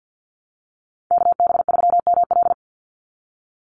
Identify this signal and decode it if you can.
Morse code, heard as a keyed tone, about 50 wpm, 700 Hz. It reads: KB3ML
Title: Call Signs - 50wpm